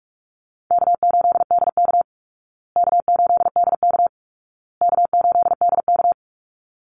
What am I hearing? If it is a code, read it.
Morse code, heard as a keyed tone, about 44 wpm, 700 Hz. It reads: K8DK K8DK K8DK
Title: Easy Call Signs Repeated 3x - 45wpm